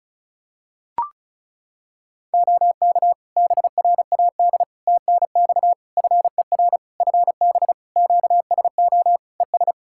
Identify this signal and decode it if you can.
Morse code, heard as a keyed tone, about 34 wpm, 700 Hz. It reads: OK BRAD TNX FER FB QSO ES
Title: K1USN SST QSO Practice 35wpm